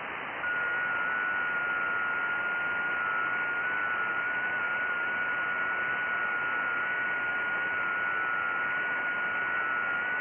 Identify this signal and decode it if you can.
FT8.
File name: IC-7100